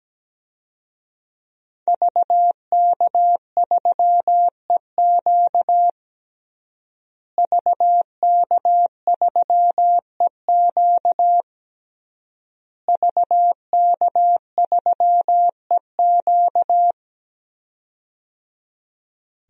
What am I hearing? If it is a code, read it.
Morse code, heard as a keyed tone, about 17 wpm, 700 Hz. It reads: VK3EQ VK3EQ VK3EQ
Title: DXCC-100 Call Signs - 5 Characters - Repeated 3x - 17wpm